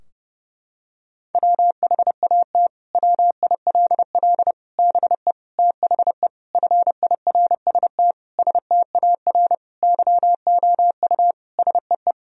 Morse code, heard as a keyed tone, about 32 wpm, 700 Hz. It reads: WHAT WILL BE THE FIRST STAR YOU SEE
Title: Sentences from Top 1000 Words - Review Part 2 - 30wpm